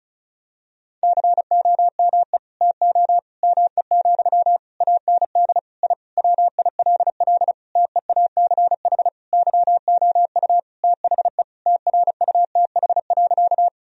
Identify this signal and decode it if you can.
Morse code, heard as a keyed tone, about 34 wpm, 700 Hz. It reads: COME TO ME, AND I WILL TEACH YOU THE TRUTH.
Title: Sentences from Top 1000 Words - Review Part 2 - 35wpm